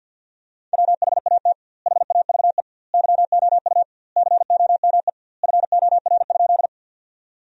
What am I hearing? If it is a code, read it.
Morse code, heard as a keyed tone, about 52 wpm, 700 Hz. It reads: WHAT HAVE YOU COME FOR?
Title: 2x Word-Spacing - Sentences from Top 100 Words - 50wpm